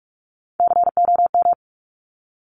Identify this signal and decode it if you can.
Morse code, heard as a keyed tone, about 46 wpm, 690 Hz. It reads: COM
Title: Top Three Letter Combinations in Words 45wpm